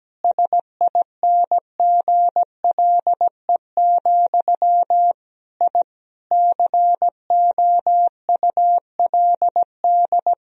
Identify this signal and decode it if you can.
Morse code, heard as a keyed tone, about 17 wpm, 700 Hz. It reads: SINGLE, I COULD
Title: Sentences from Top 1000 Words - Review Part 4 - 17wpm